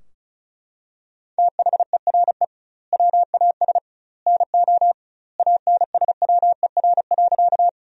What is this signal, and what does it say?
Morse code, heard as a keyed tone, about 34 wpm, 700 Hz. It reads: THERE WAS NO ANSWER.
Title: 2x Word-Spacing - Sentences from Top 300 Words - 35wpm